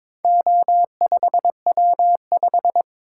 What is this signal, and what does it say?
Morse code, heard as a keyed tone, about 21 wpm, 700 Hz. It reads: O5W5
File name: practice / 22wpm / Four Letter-Number Combinations - 22wpm